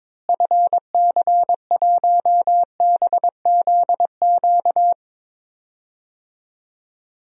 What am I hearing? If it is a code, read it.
Morse code, heard as a keyed tone, about 22 wpm, 700 Hz. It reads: FC1BZQ
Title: DXCC-100 Call Signs - 6 Characters - Repeated 2x - 22wpm